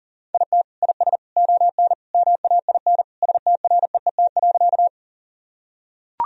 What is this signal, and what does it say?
Morse code, heard as a keyed tone, about 40 wpm, 700 Hz. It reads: IT IS ON MAIN STREET.
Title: Sentences from Top 1000 Words - Review Part 2 - 40wpm